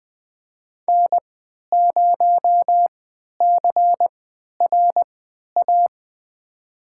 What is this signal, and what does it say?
Morse code, heard as a keyed tone, about 20 wpm, spaced as for 11 wpm, 700 Hz. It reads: N0CRA
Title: USA Call Signs - 5 Characters - 3x Character Spacing - 20wpm